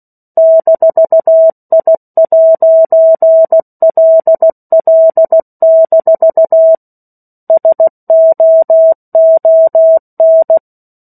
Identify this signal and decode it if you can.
Morse code, heard as a keyed tone, about 16 wpm, 650 Hz. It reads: -I'LL- SOON
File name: 16 WPM